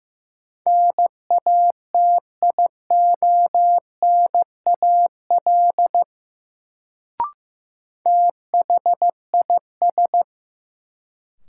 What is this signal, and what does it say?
Morse code, heard as a keyed tone, about 15 wpm, 700 Hz. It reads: NATIONAL THIS
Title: Sets of 2 Words - New Word Drill - 15wpm